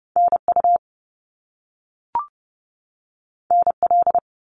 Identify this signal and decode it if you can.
Morse code, heard as a keyed tone, about 30 wpm, 700 Hz. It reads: NU NL
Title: Canadian Provinces and Territories 30wpm